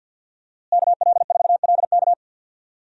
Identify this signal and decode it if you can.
Morse code, heard as a keyed tone, about 49 wpm, 700 Hz. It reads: KL4LK
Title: USA Call Signs - 5 Characters - Repeated 3x - 50wpm